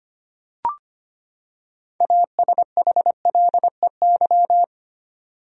Morse code, heard as a keyed tone, about 25 wpm, 700 Hz. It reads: ASHLEY
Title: USA Common Names - 25wpm